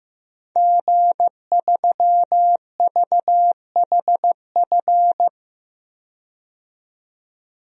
Morse code, heard as a keyed tone, about 15 wpm, 700 Hz. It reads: G3VHF
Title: DXCC-100 Call Signs - 5 Characters - 15wpm